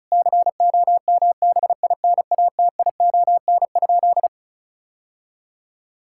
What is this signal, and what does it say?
Morse code, heard as a keyed tone, about 34 wpm, 700 Hz. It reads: COMBINATION?
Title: Sentences from Top 1000 Words - Review Part 6 - 35wpm